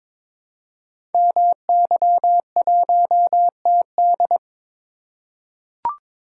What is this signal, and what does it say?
Morse code, heard as a keyed tone, about 22 wpm, 700 Hz. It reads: MY1TD
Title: Code Groups - 5 Letter:Numbers - 22wpm